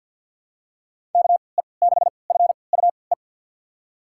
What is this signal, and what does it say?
Morse code, heard as a keyed tone, about 51 wpm, spaced as for 28 wpm, 700 Hz. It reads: KE6FUE
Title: USA Call Signs - 3x Character Spacing - 50wpm